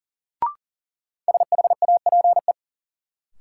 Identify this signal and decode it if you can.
Morse code, heard as a keyed tone, about 40 wpm, 700 Hz. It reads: SHAPE
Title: Sentences from Top 500 Words - New Word Drill - 40wpm